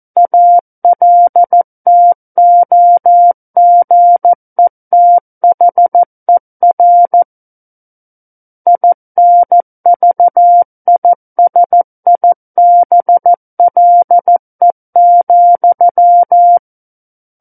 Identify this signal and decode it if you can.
Morse code, heard as a keyed tone, about 14 wpm, 700 Hz. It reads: ALTOGETHER INVISIBLE,
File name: Book One / 05-Chapter 5 - 14 WPM